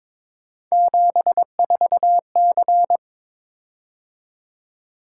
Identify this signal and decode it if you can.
Morse code, heard as a keyed tone, about 21 wpm, 700 Hz. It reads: 74C
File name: practice / 22wpm / Three Letter-Number Combinations - 22wpm